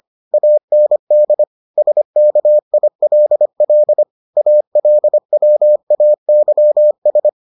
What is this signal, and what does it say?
Morse code, heard as a keyed tone, about 25 wpm, 600 Hz. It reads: AND SKILL ALWAYS